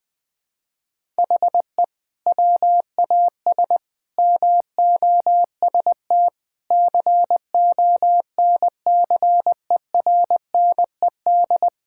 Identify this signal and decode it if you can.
Morse code, heard as a keyed tone, about 20 wpm, 700 Hz. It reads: HE WAS MOST CONCERNED
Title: Sentences from Top 1000 Words - Review Part 6 - 20wpm